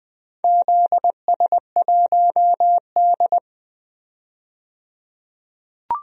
Morse code, heard as a keyed tone, about 20 wpm, 700 Hz. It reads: ZS1D
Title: DXCC-100 Call Signs - 4 Characters - 20wpm